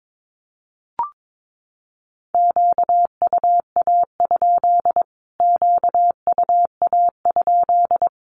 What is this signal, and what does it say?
Morse code, heard as a keyed tone, about 22 wpm, 700 Hz. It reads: QUA? QUA?
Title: Q-Codes Repeated 3x 22wpm